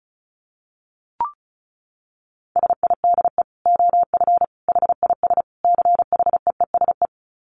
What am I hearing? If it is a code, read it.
Morse code, heard as a keyed tone, about 34 wpm, 700 Hz. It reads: SIDE OF HIS CHEESE
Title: Dit Challenge 35wpm